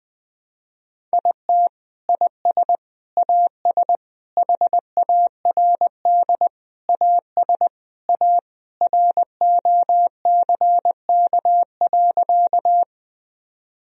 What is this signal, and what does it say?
Morse code, heard as a keyed tone, about 20 wpm, 700 Hz. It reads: IT IS AS HARD AS A ROCK.
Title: Sentences from Top 1000 Words - Review Part 2 - 20wpm